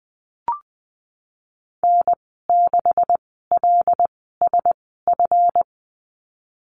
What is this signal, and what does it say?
Morse code, heard as a keyed tone, about 20 wpm, spaced as for 15 wpm, 700 Hz. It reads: N6LSF
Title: 161 Lesson 26g - Letter g - Call Signs 20wpm